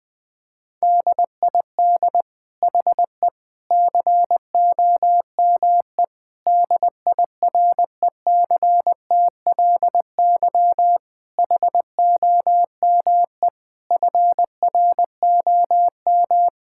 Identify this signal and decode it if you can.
Morse code, heard as a keyed tone, about 21 wpm, 700 Hz. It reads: DID HE COME DIRECTLY HOME FROM
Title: Sentences from Top 1000 Words - Review Part 5 - 20wpm